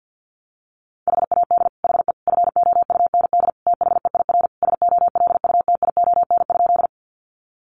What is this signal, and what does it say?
Morse code, heard as a keyed tone, about 50 wpm, 700 Hz. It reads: HAD HE FOUND THEIR SOLUTION?
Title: Sentences from Top 1300 Words 50wpm